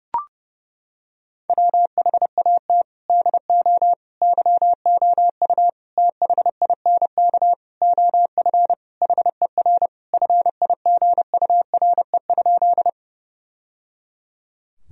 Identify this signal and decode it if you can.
Morse code, heard as a keyed tone, about 30 wpm, 700 Hz. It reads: WHAT DO YOU THINK OF HER FIGURE?
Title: Sentences from Top 1000 Words - Review Part 2 - 30wpm